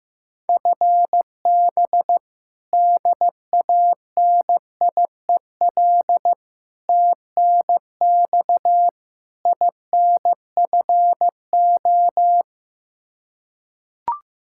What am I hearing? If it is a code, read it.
Morse code, heard as a keyed tone, about 16 wpm, 700 Hz. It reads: FB DANIEL TNX INFO
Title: Practice QSO - Exchange 3 - More Info - 15wpm